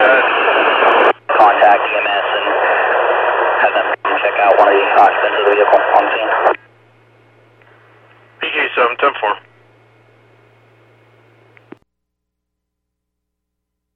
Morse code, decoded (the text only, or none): none